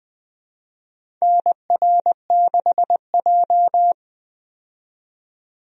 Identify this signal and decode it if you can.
Morse code, heard as a keyed tone, about 20 wpm, 700 Hz. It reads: NR6J
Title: USA Call Signs - 4 Characters - 20wpm